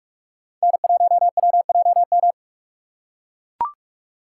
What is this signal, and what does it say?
Morse code, heard as a keyed tone, about 43 wpm, 700 Hz. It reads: N1WJM